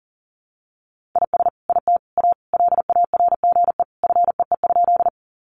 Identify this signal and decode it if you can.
Morse code, heard as a keyed tone, about 40 wpm, 700 Hz. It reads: IS IT A LARGE FEE?
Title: Sentences from Top 1500 Words 40wpm